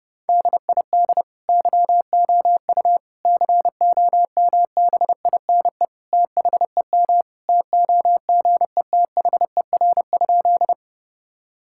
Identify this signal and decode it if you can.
Morse code, heard as a keyed tone, about 31 wpm, 700 Hz. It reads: DID YOU COMBINE THEM TOGETHER?
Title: Sentences from Top 1600 Words 30wpm